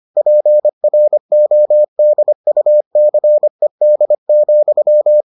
Morse code, heard as a keyed tone, about 25 wpm, 600 Hz. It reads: PRODUCED,